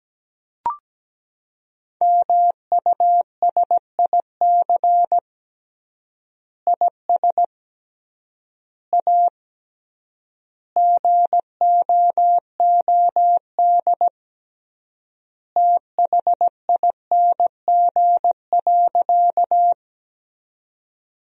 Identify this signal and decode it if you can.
Morse code, heard as a keyed tone, about 17 wpm, 700 Hz. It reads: MUSIC IS A GOOD THING.
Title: 3x Word-Spacing - Sentences from Top 300 Words - 17wpm